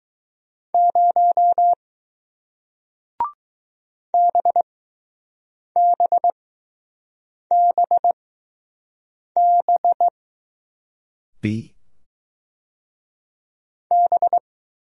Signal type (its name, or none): Morse code